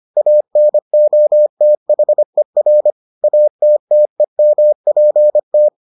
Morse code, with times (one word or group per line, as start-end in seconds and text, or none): 0.2-2.9 ANOTHER
3.2-5.7 ATTEMPT